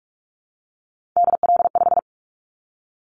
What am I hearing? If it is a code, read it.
Morse code, heard as a keyed tone, about 46 wpm, 700 Hz. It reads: DL5
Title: Callsign Prefixes for Top 50 DXCC - 45wpm